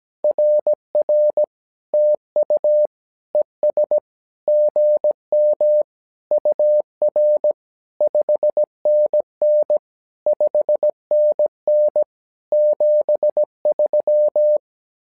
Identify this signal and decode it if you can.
Morse code, heard as a keyed tone, about 17 wpm, 600 Hz. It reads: RR TU ES GM UR 5NN 5NN 73
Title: SOTA QSO Practice 17wpm